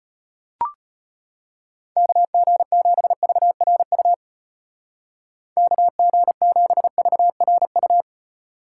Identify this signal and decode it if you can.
Morse code, heard as a keyed tone, about 32 wpm, 700 Hz. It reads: KG7VRU KG7VRU